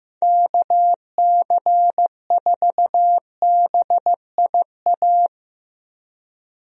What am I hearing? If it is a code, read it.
Morse code, heard as a keyed tone, about 15 wpm, 700 Hz. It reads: KC4BIA
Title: USA Call Signs - 6 Characters - Repeated 2x - 15wpm